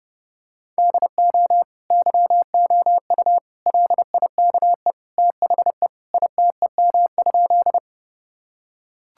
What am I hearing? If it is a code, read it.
Morse code, heard as a keyed tone, about 30 wpm, 700 Hz. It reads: DO YOU LIKE THE ITEM?
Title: Sentences from Top 1000 Words - Review Part 6 - 30wpm